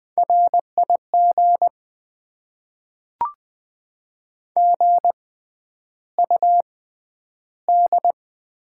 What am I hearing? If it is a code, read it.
Morse code, heard as a keyed tone, about 20 wpm, spaced as for 10 wpm, 700 Hz. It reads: RIG GUD
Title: Top 20 QSO Elements - 6x Character Spacing - 20wpm